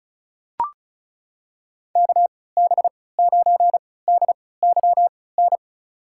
Morse code, heard as a keyed tone, about 36 wpm, spaced as for 20 wpm, 700 Hz. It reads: KB9DYN
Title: USA Call Signs - 6 Characters - 3x Character Spacing - 35wpm